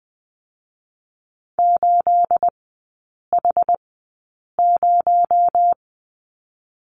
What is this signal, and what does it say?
Morse code, heard as a keyed tone, about 20 wpm, spaced as for 8 wpm, 700 Hz. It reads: 8H0
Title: 218 Lesson 35e - Number 0 - Three Characters 20wpm